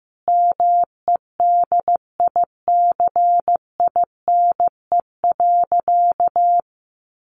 Morse code, heard as a keyed tone, about 15 wpm, 700 Hz. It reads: MEDICINE.
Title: Sentences from Top 900 Words 15wpm